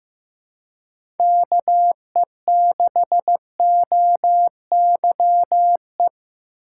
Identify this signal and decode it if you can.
Morse code, heard as a keyed tone, about 15 wpm, 700 Hz. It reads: KE6OYE
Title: USA Call Signs - 6 Characters - 2x Character Spacing - 15wpm